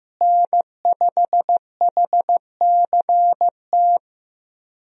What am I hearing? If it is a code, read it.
Morse code, heard as a keyed tone, about 16 wpm, 700 Hz. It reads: N5HCT
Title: USA Call Signs - 5 Characters - Repeated 3x - 15wpm